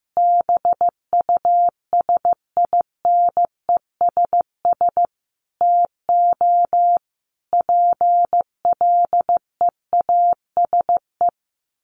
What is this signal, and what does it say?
Morse code, heard as a keyed tone, about 14 wpm, 700 Hz. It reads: BUSINESS TO PLEASE
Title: Sentences from Top 1500 Words 15wpm